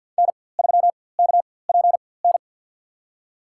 Morse code, heard as a keyed tone, about 47 wpm, spaced as for 23 wpm, 700 Hz. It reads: N3KPN
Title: USA Call Signs - 4x Character Spacing - 50wpm